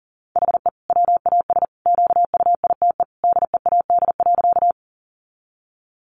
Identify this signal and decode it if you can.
Morse code, heard as a keyed tone, about 37 wpm, 700 Hz. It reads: HE WAS QUITE DEAD.
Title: Sentences from Top 1500 Words 40wpm